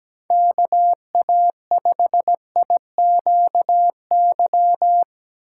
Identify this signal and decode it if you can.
Morse code, heard as a keyed tone, about 17 wpm, 700 Hz. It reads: KA5IQY
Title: USA Call Signs - 2x Character Spacing - 17wpm